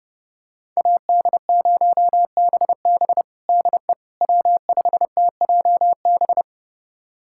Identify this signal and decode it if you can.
Morse code, heard as a keyed tone, about 30 wpm, 700 Hz. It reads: AD0BB DE W5TJB
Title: USA Callsign DE Callsign - 5 Characters - 30wpm